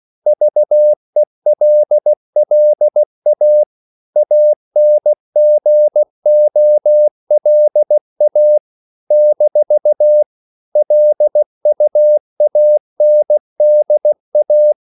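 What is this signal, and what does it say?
Morse code, heard as a keyed tone, about 16 wpm, 600 Hz. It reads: VELLA ANGOLA - LUANDA
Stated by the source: CW practice of national capitals at 16 words per minute, from ON6ZQ.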